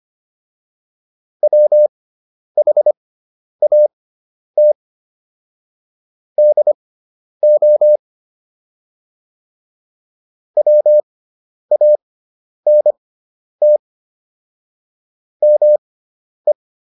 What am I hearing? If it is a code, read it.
Morse code, heard as a keyed tone, about 25 wpm, spaced as for 10 wpm, 600 Hz. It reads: WHAT DO WANT ME